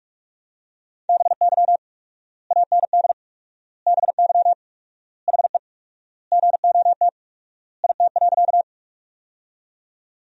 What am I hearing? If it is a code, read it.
Morse code, heard as a keyed tone, about 46 wpm, 700 Hz. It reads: BY AND BY HE GOT IT.
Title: 4x Word-Spacing - Sentences from Top 300 Words - 45wpm